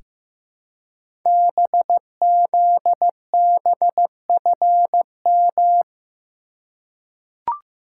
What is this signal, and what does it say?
Morse code, heard as a keyed tone, about 15 wpm, 700 Hz. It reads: BZBFM
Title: Code Groups - 5 Letters - 15wpm